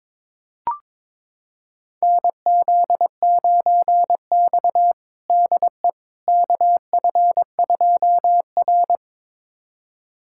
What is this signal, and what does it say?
Morse code, heard as a keyed tone, about 22 wpm, 700 Hz. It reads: NZ9X DE KF2R
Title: USA Callsign DE Callsign - 4 Characters - 22wpm